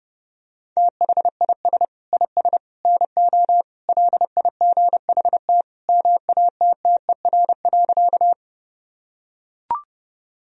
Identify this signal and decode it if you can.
Morse code, heard as a keyed tone, about 30 wpm, 700 Hz. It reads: THIS IS NO LIGHT MATTER.
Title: Sentences from Top 1000 Words - Review Part 1 - 30wpm